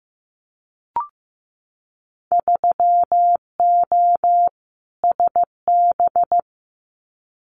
Morse code, heard as a keyed tone, about 15 wpm, 700 Hz. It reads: 3O SB
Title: Winter Field Day Exchanges Encore 15wpm